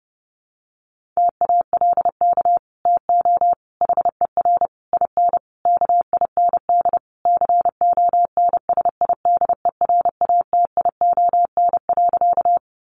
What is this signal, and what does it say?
Morse code, heard as a keyed tone, about 30 wpm, 700 Hz. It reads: TALK TO HER IN KIND CONSIDERATION.
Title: Sentences from Top 1500 Words 30wpm